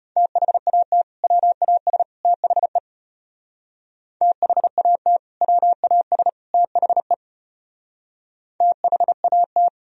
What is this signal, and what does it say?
Morse code, heard as a keyed tone, about 32 wpm, 700 Hz. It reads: THAT WAS THE THAT WAS THE THAT